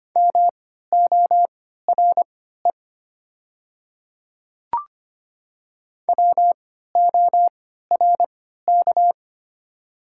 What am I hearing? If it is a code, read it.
Morse code, heard as a keyed tone, about 25 wpm, spaced as for 14 wpm, 700 Hz. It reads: MORE WORK
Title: 3x Character Spacing - Top 200 Words - 25wpm